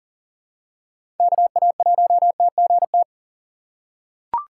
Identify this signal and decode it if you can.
Morse code, heard as a keyed tone, about 40 wpm, 700 Hz. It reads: KA1TGT
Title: USA Call Signs - 6 Characters - 3x Character Spacing - 40wpm